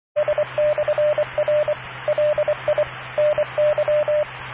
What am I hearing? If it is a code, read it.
Morse code, heard as a keyed tone, about 24 wpm, 600 Hz, with radio noise. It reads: S/R LI NY